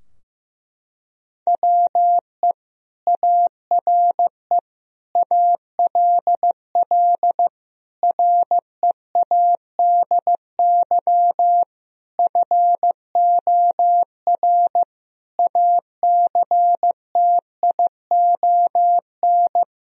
Morse code, heard as a keyed tone, about 15 wpm, 700 Hz. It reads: WE ARE ALL READY FOR ACTION
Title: Sentences from Top 1000 Words - Review Part 4 - 15wpm